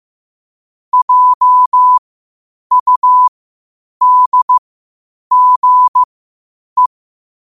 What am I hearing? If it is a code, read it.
Morse code, heard as a keyed tone, about 15 wpm, spaced as for 8 wpm, 1000 Hz. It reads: JUDGE
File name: qotdfiles / 8wpm